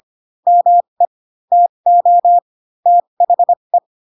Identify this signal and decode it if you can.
Morse code, heard as a keyed tone, about 25 wpm, spaced as for 22 wpm, 700 Hz. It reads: ME TO THE